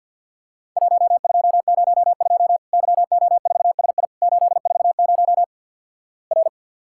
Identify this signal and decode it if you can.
Morse code, heard as a keyed tone, about 48 wpm, 700 Hz. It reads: 1202 YO4SI 840 R
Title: Worked All Europe DX Contest - QTC Exchange with Repeats - 50wpm